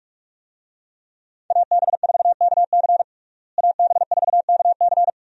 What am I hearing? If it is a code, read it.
Morse code, heard as a keyed tone, about 46 wpm, 700 Hz. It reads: AB4KC AB4KC
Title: USA Call Signs - 5 Characters - Repeated 2x - 45wpm